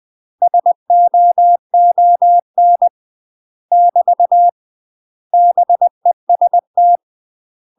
Morse code, heard as a keyed tone, about 20 wpm, 700 Hz. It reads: SOON <BT> BEST
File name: UK MASTER QSOs TRAINING / 20WPM / UK MASTER QSOs 0069